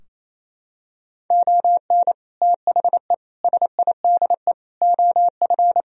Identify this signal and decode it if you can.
Morse code, heard as a keyed tone, about 28 wpm, 700 Hz. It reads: ON THE SIDE OF
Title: Sentences from Top 1000 Words - Review Part 3 - 28wpm